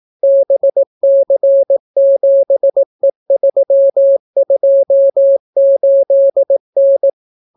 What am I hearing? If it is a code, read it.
Morse code, heard as a keyed tone, about 18 wpm, 550 Hz. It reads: BC7E328N